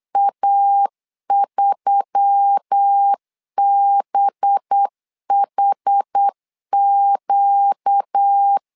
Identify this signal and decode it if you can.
Morse code, heard as a keyed tone, about 8 wpm, 800 Hz. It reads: A3BHQ
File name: LEZIONI / 5 LBVC / 5 LB 8wpm 2816